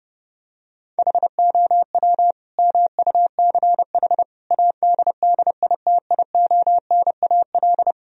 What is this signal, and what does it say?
Morse code, heard as a keyed tone, about 30 wpm, 700 Hz. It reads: HOW MUCH ADDITIONAL
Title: Sentences from Top 1000 Words - Review Part 4 - 30wpm